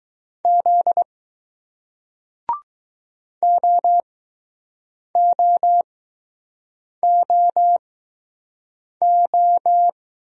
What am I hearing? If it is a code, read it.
Morse code, heard as a keyed tone, about 20 wpm, spaced as for 6 wpm, 700 Hz. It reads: Z OOOO